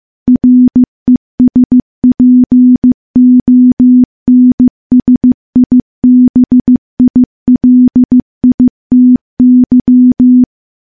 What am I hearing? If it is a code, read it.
Morse code, heard as a keyed tone, about 15 wpm, 250 Hz. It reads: RESPONSIBILITY
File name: morse-training-15WPM